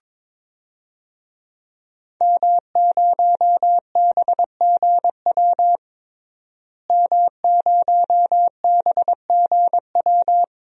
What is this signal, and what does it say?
Morse code, heard as a keyed tone, about 22 wpm, 700 Hz. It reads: M0BGW M0BGW
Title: DXCC-100 Call Signs - 5 Characters - Repeated 2x - 22wpm